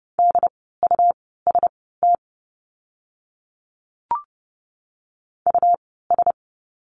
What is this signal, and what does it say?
Morse code, heard as a keyed tone, about 30 wpm, spaced as for 17 wpm, 700 Hz. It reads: DUST US
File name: Lesson 17f - Letter u - Words 30wpm